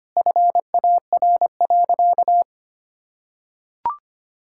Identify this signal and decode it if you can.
Morse code, heard as a keyed tone, about 25 wpm, 700 Hz. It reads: FAR.
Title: Sentences from Top 1000 Words - Review Part 5 - 25wpm